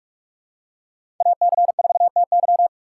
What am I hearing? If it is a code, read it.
Morse code, heard as a keyed tone, about 44 wpm, 690 Hz. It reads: AC4TY
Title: USA Call Signs - 5 Characters - 8x Character Spacing - 45wpm